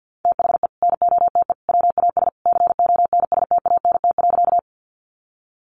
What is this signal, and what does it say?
Morse code, heard as a keyed tone, about 52 wpm, 700 Hz. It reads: THE NOTE WAS CONSTANT.
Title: Sentences from Top 1500 Words 50wpm